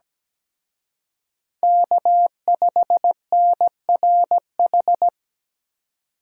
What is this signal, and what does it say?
Morse code, heard as a keyed tone, about 20 wpm, 700 Hz. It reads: K5NRH
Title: Speed Racing - USA 1x3 Call Signs 17wpm